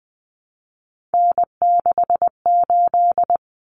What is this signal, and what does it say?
Morse code, heard as a keyed tone, about 20 wpm, 700 Hz. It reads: N68
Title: Common Cut Numbers - 3-Digit 20wpm